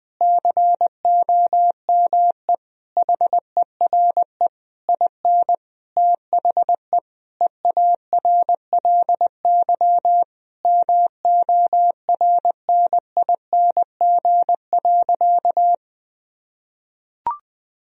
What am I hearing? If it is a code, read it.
Morse code, heard as a keyed tone, about 20 wpm, 700 Hz. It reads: COME HERE IN THE EARLY MORNING.
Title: Sentences from Top 1000 Words - Review Part 3 - 20wpm